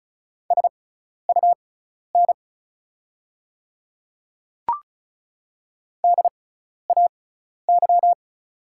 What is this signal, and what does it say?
Morse code, heard as a keyed tone, about 36 wpm, spaced as for 12 wpm, 690 Hz. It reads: SUN DAY
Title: 6x Character Spacing - Top 300 Words - 35wpm